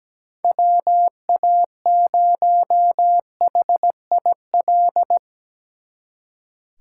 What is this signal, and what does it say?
Morse code, heard as a keyed tone, about 20 wpm, 700 Hz. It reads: WA0HIL
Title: Speed Racing - USA Call Signs 17wpm